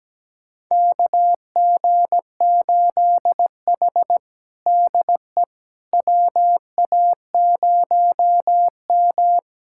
Morse code, heard as a keyed tone, about 17 wpm, 700 Hz. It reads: KG8H DE WA0M
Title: USA Callsign DE Callsign - 4 Characters - 17wpm